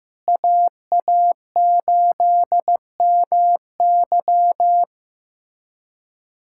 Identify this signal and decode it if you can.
Morse code, heard as a keyed tone, about 15 wpm, 700 Hz. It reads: AA8MY
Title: USA Callsign DE Callsign - 5 Characters - 15wpm